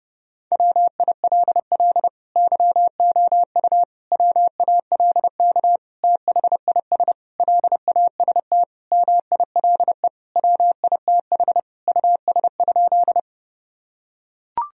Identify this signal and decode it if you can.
Morse code, heard as a keyed tone, about 32 wpm, 700 Hz. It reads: WILL YOU WALK THIS LAST MILE WITH US?
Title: Sentences from Top 1000 Words - Review Part 1 - 30wpm